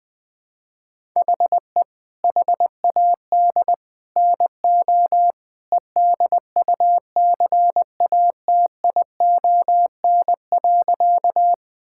Morse code, heard as a keyed tone, about 21 wpm, 700 Hz. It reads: HE HAD NO EDUCATION.
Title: Sentences from Top 1000 Words - Review Part 6 - 20wpm